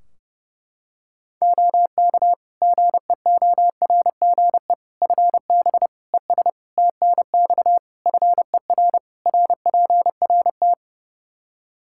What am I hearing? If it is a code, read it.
Morse code, heard as a keyed tone, about 32 wpm, 700 Hz. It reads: OK GEORGE FB ES TNX FER RPRT
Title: K1USN SST QSO Practice 30wpm